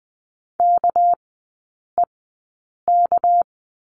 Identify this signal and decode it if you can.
Morse code, heard as a keyed tone, about 20 wpm, spaced as for 8 wpm, 700 Hz. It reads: KEK
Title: 198 Lesson 32e - Letter k - Three Characters 20wpm